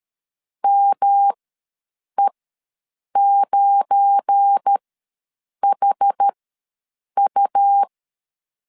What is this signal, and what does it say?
Morse code, heard as a keyed tone, about 13 wpm, spaced as for 7 wpm, 800 Hz. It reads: ME9HU
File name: LEZIONI / 7 GF9Z / 7 GF9 ws 12wpm 3344